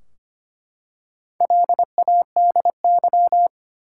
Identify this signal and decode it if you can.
Morse code, heard as a keyed tone, about 25 wpm, 700 Hz. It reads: LADY
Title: Four Letter Words - 25wpm